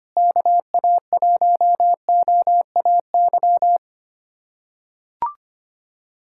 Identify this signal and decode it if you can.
Morse code, heard as a keyed tone, about 25 wpm, 700 Hz. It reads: KA1OAY
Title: USA Callsign DE Callsign 25wpm